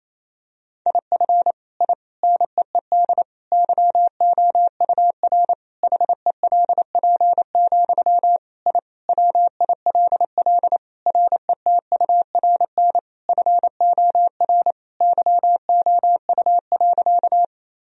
Morse code, heard as a keyed tone, about 28 wpm, 700 Hz. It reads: IF I NEED YOUR HELP, I WILL RETURN FOR YOU.
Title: Sentences from Top 1000 Words - Review Part 1 - 28wpm